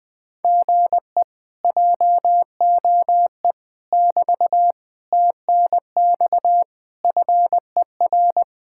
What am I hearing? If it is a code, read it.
Morse code, heard as a keyed tone, about 20 wpm, 700 Hz. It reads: GE JOE <BT> TNX FER
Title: Speed-Racing Common QSO Phrases - 20wpm